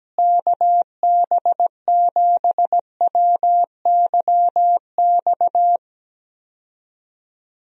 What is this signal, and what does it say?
Morse code, heard as a keyed tone, about 17 wpm, 700 Hz. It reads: KB7WYX
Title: USA Call Signs - 6 Characters - Repeated 2x - 17wpm